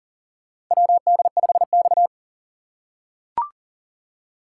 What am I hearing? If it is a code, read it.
Morse code, heard as a keyed tone, about 40 wpm, 700 Hz. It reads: WD5X
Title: USA Call Signs - 4 Characters - 6x Character Spacing - 40wpm